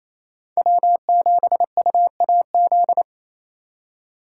Morse code, heard as a keyed tone, about 35 wpm, 700 Hz. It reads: W7UAZ
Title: Speed Racing - USA 1x3 Call Signs 28wpm